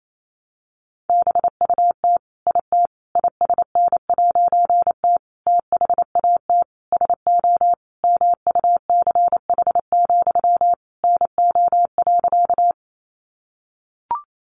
Morse code, heard as a keyed tone, about 28 wpm, 700 Hz. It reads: BUT IT ISN'T THAT SO MUCH, NO.
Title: A Sound Of Thunder 28wpm